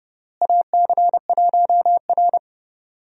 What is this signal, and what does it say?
Morse code, heard as a keyed tone, about 30 wpm, 700 Hz. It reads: AC1R
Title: USA Call Signs - 4 Characters - Repeated 3x - 30wpm